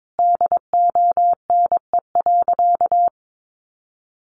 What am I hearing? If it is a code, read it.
Morse code, heard as a keyed tone, about 22 wpm, 700 Hz. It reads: DONE.
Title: Sentences from Top 700 Words 22wpm